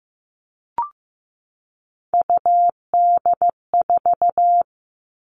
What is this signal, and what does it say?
Morse code, heard as a keyed tone, about 15 wpm, 700 Hz. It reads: UD4
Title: All Cut Numbers - 3-Digit 15wpm